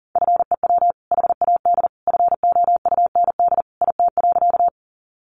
Morse code, heard as a keyed tone, about 40 wpm, 700 Hz. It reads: FEW HAD FOUND IT.
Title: Sentences from Top 300 Words 40wpm